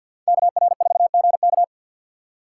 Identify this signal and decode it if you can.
Morse code, heard as a keyed tone, about 53 wpm, 700 Hz. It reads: KR4GK
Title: USA Call Signs - 5 Characters - 4x Character Spacing - 50wpm